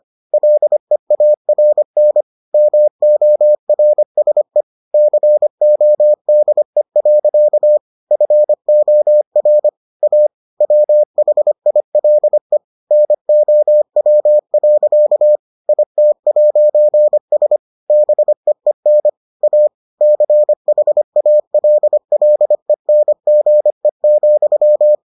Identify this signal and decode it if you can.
Morse code, heard as a keyed tone, about 25 wpm, 600 Hz. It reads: LEARN MORSE CODE. FOR A WHILE NOW. IT'S BEEN A CHALLENGE,